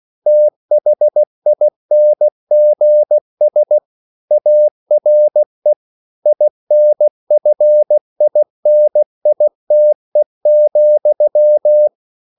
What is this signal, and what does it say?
Morse code, heard as a keyed tone, about 16 wpm, 600 Hz. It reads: THINGS ARE INFINITE,
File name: CW / Practice / EnglishQuotations0116